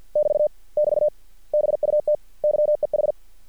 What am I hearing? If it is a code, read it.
Morse code, heard as a keyed tone, about 50 wpm, 600 Hz. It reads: <BT> <BT> BUT YEH
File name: archived / CRSnet